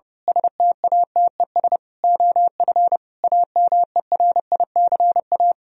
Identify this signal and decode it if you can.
Morse code, heard as a keyed tone, about 28 wpm, 700 Hz. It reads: STATES OF AMERICA
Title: Speed-Racing - Sets of 5 Words 30wpm